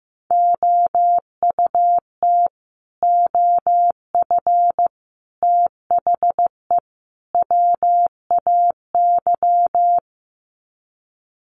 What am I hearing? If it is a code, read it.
Morse code, heard as a keyed tone, about 15 wpm, 700 Hz. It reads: OUT OF THE WAY
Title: Sets of 4 Words 15wpm